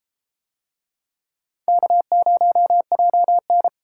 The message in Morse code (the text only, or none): K0JN